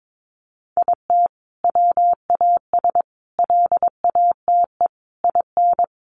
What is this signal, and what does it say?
Morse code, heard as a keyed tone, about 22 wpm, 700 Hz. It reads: IT WAS LATE IN
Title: Sentences from Top 1300 Words 22wpm